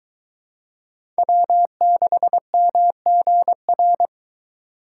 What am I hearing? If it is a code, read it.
Morse code, heard as a keyed tone, about 22 wpm, 700 Hz. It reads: W6MGR